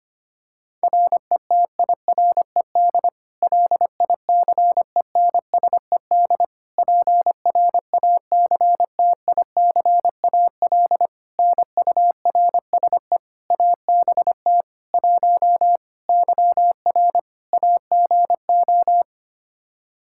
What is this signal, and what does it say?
Morse code, heard as a keyed tone, about 25 wpm, 700 Hz. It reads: RETIRED LICENSED PRACTICAL NURSE ABT 1 YR AGO
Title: Practice QSO - Exchange 3 - More Info - 25wpm